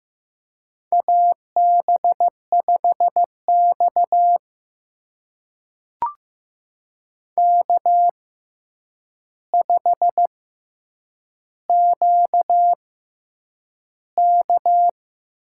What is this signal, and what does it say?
Morse code, heard as a keyed tone, about 15 wpm, spaced as for 8 wpm, 700 Hz. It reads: AB5X K5QK
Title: USA Call Signs - 4 Characters - 6x Character Spacing - 15wpm